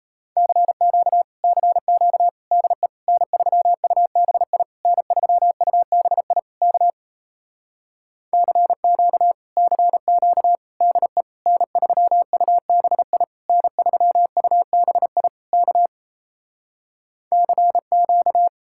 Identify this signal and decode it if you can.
Morse code, heard as a keyed tone, about 37 wpm, 700 Hz. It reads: CQ CQ DE N3UBI N3UBI K CQ CQ DE N3UBI N3UBI K CQ